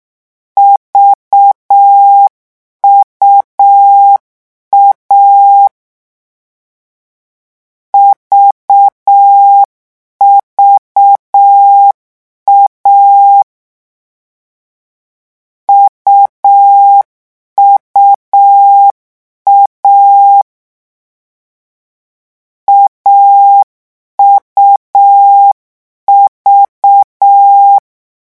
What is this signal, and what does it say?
Morse code, heard as a keyed tone, about 6 wpm, 800 Hz. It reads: VUA VVA UUA AUV
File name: morse mp3 / lecon3-vitesse 6